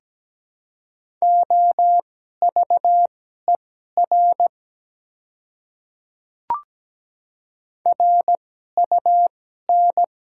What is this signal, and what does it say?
Morse code, heard as a keyed tone, about 17 wpm, spaced as for 12 wpm, 690 Hz. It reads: OVER RUN
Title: 2x Character Spacing - Top 300 Words - 17wpm